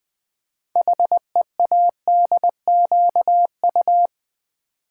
Morse code, heard as a keyed tone, about 20 wpm, 700 Hz. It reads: HEADQU
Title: Build A Word - 6 Letters At A Time - 12-Letter Words - 20wpm